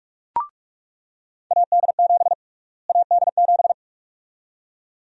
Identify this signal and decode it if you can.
Morse code, heard as a keyed tone, about 42 wpm, 700 Hz. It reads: AD7 AD7
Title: USA Call Sign Prefixes - Repeated 2x - 45wpm